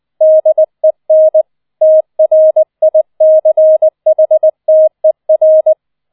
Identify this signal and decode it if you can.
Morse code, heard as a keyed tone, about 20 wpm, 630 Hz. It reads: DEN TRICHTER
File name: 007 Letzter Streich 20wpm